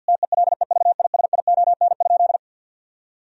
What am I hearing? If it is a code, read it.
Morse code, heard as a keyed tone, about 50 wpm, 700 Hz. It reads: TELEVISION?
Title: Sentences from Top 1000 Words - Review Part 4 - 50wpm